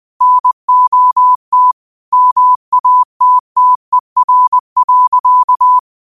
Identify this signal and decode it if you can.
Morse code, heard as a keyed tone, about 20 wpm, 1000 Hz. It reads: NOT MATTER.
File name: qotdfiles / 20wpm